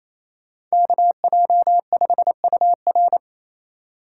Morse code, heard as a keyed tone, about 36 wpm, 700 Hz. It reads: KJ5UR